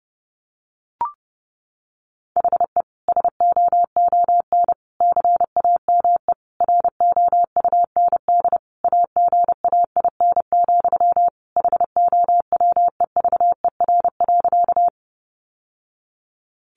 Morse code, heard as a keyed tone, about 30 wpm, 700 Hz. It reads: HE SOON CAME ROUND AGAIN, HOWEVER.
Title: Sentences from Top 600 Words 30wpm